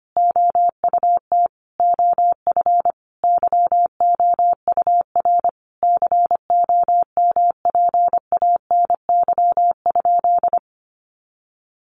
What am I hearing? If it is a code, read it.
Morse code, heard as a keyed tone, about 25 wpm, 700 Hz. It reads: OUT OF YOUR COMPANY?
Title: Sentences from Top 600 Words 25wpm